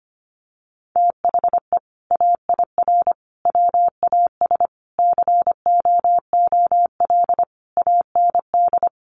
Morse code, heard as a keyed tone, about 25 wpm, 700 Hz. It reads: THE AIR WAS COOL AND
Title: Sentences from Top 1200 Words 25wpm